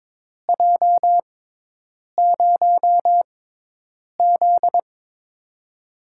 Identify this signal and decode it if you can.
Morse code, heard as a keyed tone, about 22 wpm, spaced as for 8 wpm, 700 Hz. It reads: J0Z
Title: USA Call Signs - 4 Characters - 6x Character Spacing - 22wpm